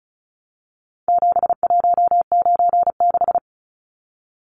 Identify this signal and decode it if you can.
Morse code, heard as a keyed tone, about 34 wpm, 700 Hz. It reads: 7196
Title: Numbers - 4-Digit 35wpm